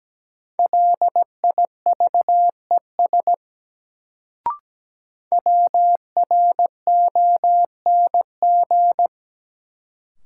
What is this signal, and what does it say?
Morse code, heard as a keyed tone, about 17 wpm, 700 Hz. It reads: LIVES WRONG
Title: Sets of 5 Words Encore - New Word Drill - 17wpm